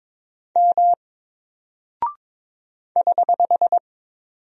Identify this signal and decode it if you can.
Morse code, heard as a keyed tone, about 21 wpm, 700 Hz. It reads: M <HH>
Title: Single Character (Letter:Number:Punctuation:Prosign) Alt 22wpm